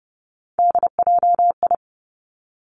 Morse code, heard as a keyed tone, about 30 wpm, 700 Hz. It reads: DJI
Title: Three Challenging Letter-Number Combinations 30wpm